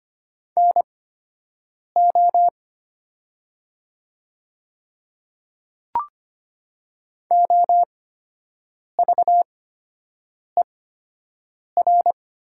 Morse code, heard as a keyed tone, about 25 wpm, spaced as for 7 wpm, 700 Hz. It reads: NO OVER
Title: 8x Character Spacing - Top 100 Words - 25wpm